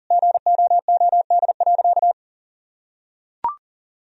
Morse code, heard as a keyed tone, about 40 wpm, 700 Hz. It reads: GOOD.
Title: Sentences from Top 1000 Words - Review Part 3 - 40wpm